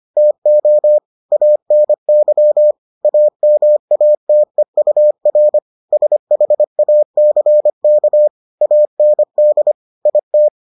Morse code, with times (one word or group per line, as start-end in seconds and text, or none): 0.2-1.0 TO
1.3-2.7 ANY
3.1-5.6 AMATEUR
5.9-8.3 SHACK
8.6-9.7 AND
10.1-10.5 IT